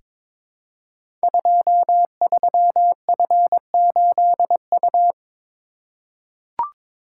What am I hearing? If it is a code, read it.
Morse code, heard as a keyed tone, about 22 wpm, 700 Hz. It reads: 23F8U
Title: Code Groups - 5 Letter:Numbers - 22wpm